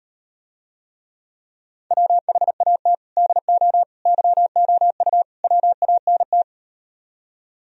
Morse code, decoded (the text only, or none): WHAT DO YOU WANT